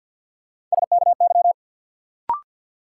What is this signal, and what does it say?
Morse code, heard as a keyed tone, about 52 wpm, 700 Hz. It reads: SKY
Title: Sentences from Top 1200 Words - New Word Drill Alt - 50wpm